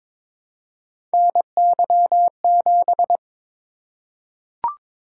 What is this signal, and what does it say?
Morse code, heard as a keyed tone, about 22 wpm, 700 Hz. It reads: NY7
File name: practice / 22wpm / Three Letter-Number Combinations - 22wpm